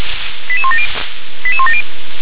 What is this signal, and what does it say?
Five-tone selective call (ZVEI), heard as quick runs of five tones, tones of about 70 ms. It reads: 88170 88170